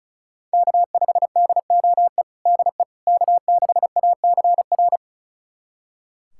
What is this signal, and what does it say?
Morse code, heard as a keyed tone, about 36 wpm, 700 Hz. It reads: K5DOE DE K6ACR
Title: USA Callsign DE Callsign - 5 Characters - 35wpm